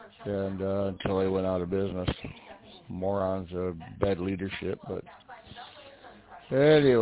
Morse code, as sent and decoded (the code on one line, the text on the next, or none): none